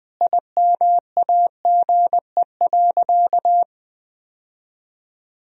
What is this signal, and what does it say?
Morse code, heard as a keyed tone, about 21 wpm, 700 Hz. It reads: IMAGE.
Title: Sentences from Top 1000 Words - Review Part 4 - 20wpm